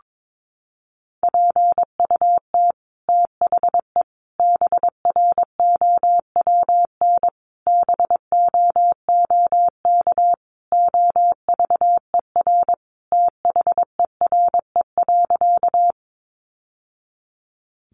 Morse code, heard as a keyed tone, about 22 wpm, 700 Hz. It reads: PUT THE BROWN BOOK OVER THERE.
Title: Sentences from Top 1200 Words 22wpm